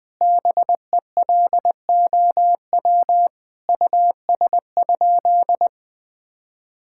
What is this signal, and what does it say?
Morse code, heard as a keyed tone, about 20 wpm, 700 Hz. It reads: BELOW US?
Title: Sentences from Top 1000 Words - Review Part 5 - 20wpm